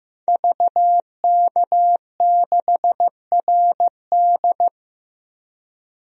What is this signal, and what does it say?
Morse code, heard as a keyed tone, about 15 wpm, 700 Hz. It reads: VK6RD
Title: DXCC-100 Call Signs - 5 Characters - Repeated 3x - 15wpm